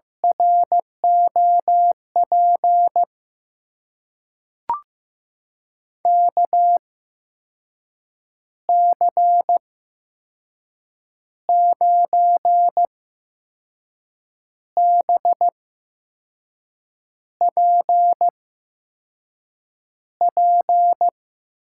Morse code, heard as a keyed tone, about 16 wpm, spaced as for 5 wpm, 700 Hz. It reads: ROP KC9BPP
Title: USA Call Signs - 6 Characters - 8x Character Spacing - 15wpm